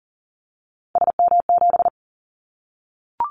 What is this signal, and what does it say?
Morse code, heard as a keyed tone, about 39 wpm, 700 Hz. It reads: SM7
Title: Callsign Prefixes for Top 25 DXCC - 40wpm